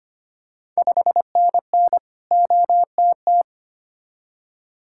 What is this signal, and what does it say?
Morse code, heard as a keyed tone, about 25 wpm, 690 Hz. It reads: 5NN OTT
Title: Ontario QSO Party Exchange 25wpm